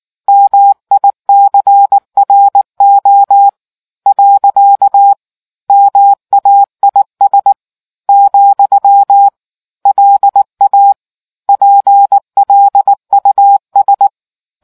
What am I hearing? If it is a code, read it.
Morse code, heard as a keyed tone, about 19 wpm, 800 Hz. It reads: MICRO . MAIS , LA PLUS